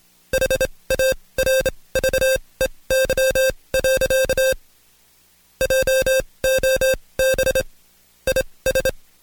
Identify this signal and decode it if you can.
Morse code, heard as a keyed tone, about 26 wpm, spaced as for 20 wpm, 530 Hz. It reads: HARVEY. JOB IS
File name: Code Quick / AudioDownld2 / Seg136-Stepper9 19WPM